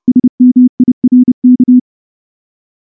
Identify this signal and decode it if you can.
Morse code, heard as a keyed tone, about 28 wpm, 250 Hz. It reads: SMIRK
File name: morse / morse-training-30WPM